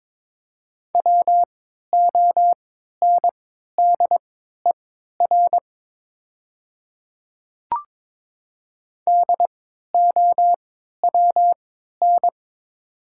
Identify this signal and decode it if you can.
Morse code, heard as a keyed tone, about 22 wpm, spaced as for 12 wpm, 700 Hz. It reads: WONDER DOWN
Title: 3x Character Spacing - Top 500 Words - 22wpm